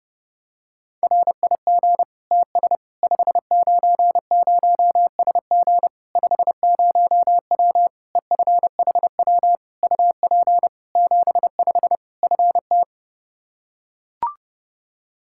Morse code, heard as a keyed tone, about 30 wpm, 700 Hz. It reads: RIG TS 590SG 50W EFHW UP 75 FT
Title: Practice QSO - Exchange 2 - Info - 30wpm